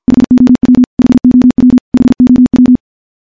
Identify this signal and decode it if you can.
Morse code, heard as a keyed tone, about 45 wpm, 250 Hz. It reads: HOW HOW HOW
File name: morse / morse-training-50WPM